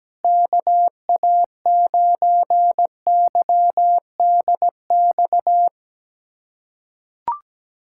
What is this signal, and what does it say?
Morse code, heard as a keyed tone, about 17 wpm, 700 Hz. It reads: KA9YDX
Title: USA Callsign DE Callsign 17wpm